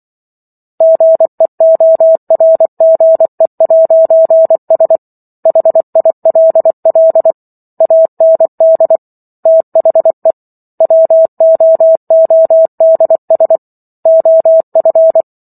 Morse code, heard as a keyed tone, about 24 wpm, 650 Hz. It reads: GEORGE'S HILL AND THE WOODS OF
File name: Chapter 15 - 24 WPM